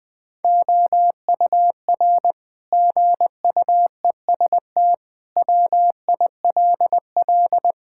Morse code, heard as a keyed tone, about 21 wpm, 700 Hz. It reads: OUR GUEST WILL
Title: Sentences from Top 1600 Words 20wpm